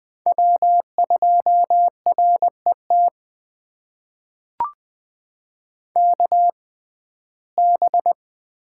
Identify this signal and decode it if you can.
Morse code, heard as a keyed tone, about 20 wpm, spaced as for 14 wpm, 700 Hz. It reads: W2RET KB
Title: USA Call Signs - 6x Character Spacing - 20wpm